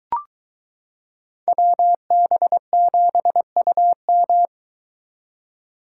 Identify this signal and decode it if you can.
Morse code, heard as a keyed tone, about 23 wpm, 700 Hz. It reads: WB7UM